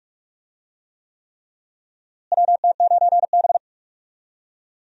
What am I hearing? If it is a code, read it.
Morse code, heard as a keyed tone, about 46 wpm, 700 Hz. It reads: WT9B
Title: USA Call Signs - 4 Characters - Repeated 3x - 45wpm